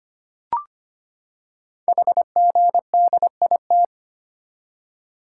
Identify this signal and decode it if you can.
Morse code, heard as a keyed tone, about 25 wpm, 700 Hz. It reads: HGDIT
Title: Code Groups - 5 Letters - 25wpm